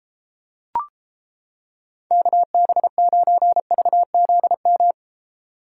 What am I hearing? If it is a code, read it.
Morse code, heard as a keyed tone, about 31 wpm, 700 Hz. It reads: KB9VZM